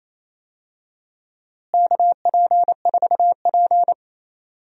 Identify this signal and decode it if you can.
Morse code, heard as a keyed tone, about 28 wpm, 700 Hz. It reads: KP4P
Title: DXCC-100 Call Signs - 4 Characters - Repeated 2x - 28wpm